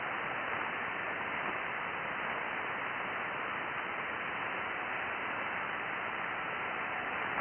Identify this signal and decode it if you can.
FT8.
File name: IC-7100